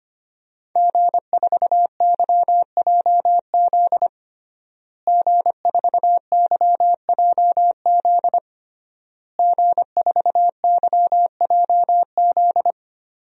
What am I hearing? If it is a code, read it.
Morse code, heard as a keyed tone, about 25 wpm, 700 Hz. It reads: G4YJZ G4YJZ G4YJZ
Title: DXCC-100 Call Signs - 5 Characters - Repeated 3x - 25wpm